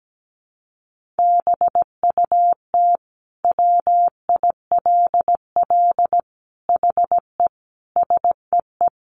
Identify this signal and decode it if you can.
Morse code, heard as a keyed tone, about 17 wpm, 700 Hz. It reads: BUT WILL HE SEE
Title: Sentences from Top 100 Words 17wpm